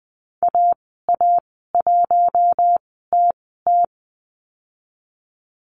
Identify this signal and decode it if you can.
Morse code, heard as a keyed tone, about 20 wpm, spaced as for 14 wpm, 700 Hz. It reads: AA1TT
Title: 042 Lesson 8g - Number 1 - Call Signs 20wpm